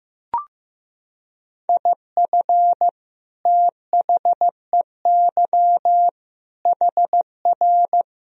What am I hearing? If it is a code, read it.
Morse code, heard as a keyed tone, about 15 wpm, 700 Hz. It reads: IF THEY HR
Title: Sentences from Top 400 Words and Abbreviations 15wpm